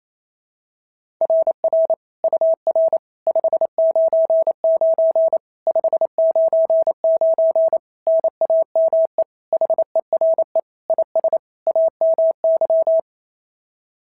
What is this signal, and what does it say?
Morse code, heard as a keyed tone, about 27 wpm, 660 Hz. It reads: RR UR 599 599 NAME HERE IS AMY
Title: SOTA QSO Practice 28wpm